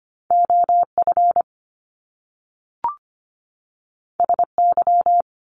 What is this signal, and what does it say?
Morse code, heard as a keyed tone, about 25 wpm, 700 Hz. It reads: OF SY
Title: Top Two Letter Combinations in Words 25wpm